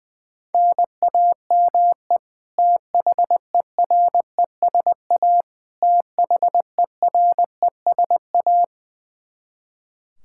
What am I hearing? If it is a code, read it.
Morse code, heard as a keyed tone, about 20 wpm, 700 Hz. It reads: NAME THERESA THERESA
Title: Practice QSO - Exchange 1 - Beginning - 20wpm